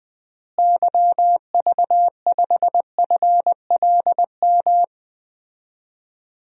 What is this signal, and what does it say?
Morse code, heard as a keyed tone, about 20 wpm, 700 Hz. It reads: YV5FLM
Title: DXCC-100 Call Signs - 6 Characters - Repeated 3x - 20wpm